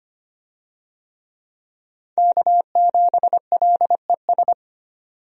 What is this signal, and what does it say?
Morse code, heard as a keyed tone, about 25 wpm, 700 Hz. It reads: K7LES
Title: USA Call Signs - 5 Characters - 25wpm